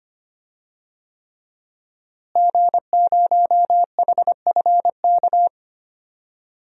Morse code, heard as a keyed tone, about 25 wpm, 700 Hz. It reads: G0HFK
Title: DXCC-100 Call Signs - 5 Characters - Repeated 3x - 25wpm